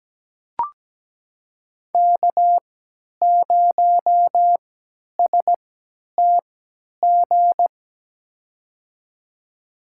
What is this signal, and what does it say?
Morse code, heard as a keyed tone, about 17 wpm, spaced as for 10 wpm, 700 Hz. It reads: K0STG
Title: USA Call Signs - 5 Characters - 3x Character Spacing - 17wpm